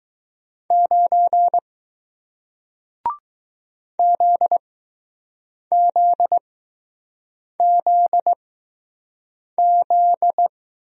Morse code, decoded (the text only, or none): 9 ZZZZ